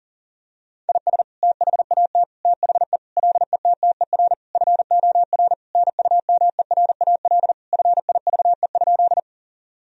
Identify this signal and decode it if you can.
Morse code, heard as a keyed tone, about 40 wpm, 700 Hz. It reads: IS THAT THE LETTER FOR NUMERAL FIVE?
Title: Sentences from Top 1000 Words - Review Part 2 - 40wpm